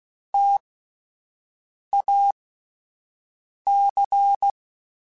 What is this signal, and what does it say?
Morse code, heard as a keyed tone, about 16 wpm, spaced as for 5 wpm, 800 Hz. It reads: TAC
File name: Alphabetical / Alphabetical mp3s / Ts @5